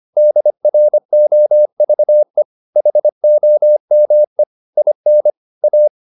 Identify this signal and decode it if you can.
Morse code, heard as a keyed tone, about 25 wpm, 600 Hz. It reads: DROVE HOME IN A